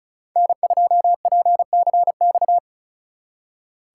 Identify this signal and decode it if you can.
Morse code, heard as a keyed tone, about 47 wpm, 700 Hz. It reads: N2PCX